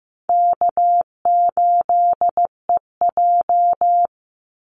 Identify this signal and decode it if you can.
Morse code, heard as a keyed tone, about 15 wpm, 700 Hz. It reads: K8EJ
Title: Easy Call Signs Repeated 3x - 15wpm